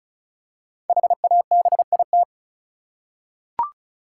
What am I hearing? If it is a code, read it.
Morse code, heard as a keyed tone, about 36 wpm, 700 Hz. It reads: HABIT
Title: Five Letter Words - 35wpm